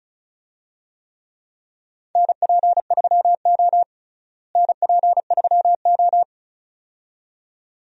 Morse code, heard as a keyed tone, about 36 wpm, 700 Hz. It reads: NP3O NP3O
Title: USA Call Signs - 4 Characters - Repeated 2x - 35wpm